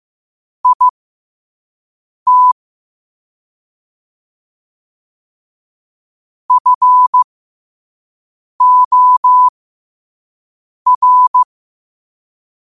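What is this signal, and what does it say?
Morse code, heard as a keyed tone, about 15 wpm, spaced as for 5 wpm, 1000 Hz. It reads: IT FOR
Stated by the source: Morse quote-of-the-day practice at 5 wpm.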